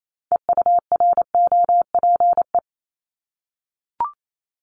Morse code, heard as a keyed tone, about 28 wpm, 700 Hz. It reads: EUROPE
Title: Multi-letter Phonemes 28wpm